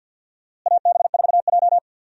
Morse code, heard as a keyed tone, about 52 wpm, 700 Hz. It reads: AB4J
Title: USA Call Signs - 4 Characters - Repeated 3x - 50wpm